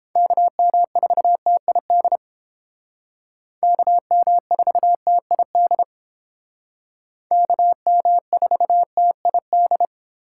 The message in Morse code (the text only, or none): KM4TID KM4TID KM4TID